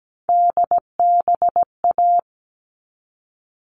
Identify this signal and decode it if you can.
Morse code, heard as a keyed tone, about 17 wpm, 700 Hz. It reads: DBA
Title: All Cut Numbers - 3-Digit 17wpm